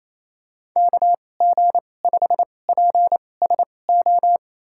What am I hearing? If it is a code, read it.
Morse code, heard as a keyed tone, about 28 wpm, spaced as for 20 wpm, 700 Hz. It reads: KG5PSO
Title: USA Call Signs - 6 Characters - 2x Character Spacing - 28wpm